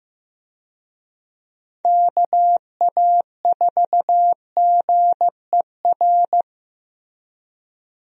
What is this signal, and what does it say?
Morse code, heard as a keyed tone, about 15 wpm, 700 Hz. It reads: KA4GER
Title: USA Call Signs - 6 Characters - Repeated 2x - 15wpm